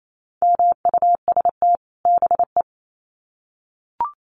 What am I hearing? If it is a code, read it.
Morse code, heard as a keyed tone, about 28 wpm, 700 Hz. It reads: MUST BE
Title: Sets of 2 Words 28wpm